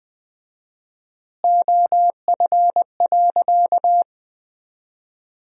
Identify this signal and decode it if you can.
Morse code, heard as a keyed tone, about 20 wpm, 700 Hz. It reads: OF.
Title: 4x Word-Spacing - Sentences from Top 300 Words - 20wpm